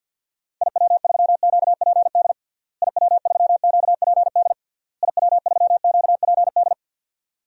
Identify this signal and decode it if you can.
Morse code, heard as a keyed tone, about 51 wpm, 700 Hz. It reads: IW3QPD IW3QPD IW3QPD
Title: DXCC-100 Call Signs - 6 Characters - Repeated 3x - 50wpm